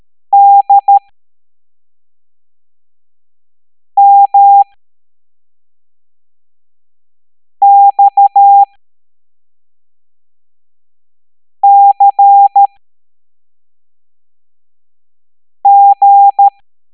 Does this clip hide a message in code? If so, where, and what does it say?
Yes: Morse code from 0.3 to 16.5 s, heard as a keyed tone, about 13 wpm, spaced as for 3 wpm, 800 Hz. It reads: DMXCG